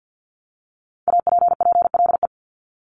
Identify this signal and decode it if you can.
Morse code, heard as a keyed tone, about 50 wpm, 700 Hz. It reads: APPLE
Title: Multi-letter Phonemes Repeated 3x 50wpm